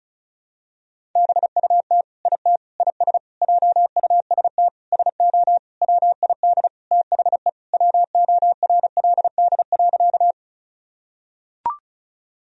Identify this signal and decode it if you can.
Morse code, heard as a keyed tone, about 35 wpm, 700 Hz. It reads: BUT IT IS JUST SO WID THE WORLD.
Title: Sentences from Top 200 Words and Abbreviations 35wpm